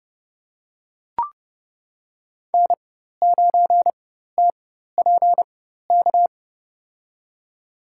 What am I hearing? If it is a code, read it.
Morse code, heard as a keyed tone, about 30 wpm, spaced as for 14 wpm, 700 Hz. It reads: N9TPK
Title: USA Call Signs - 5 Characters - 4x Character Spacing - 30wpm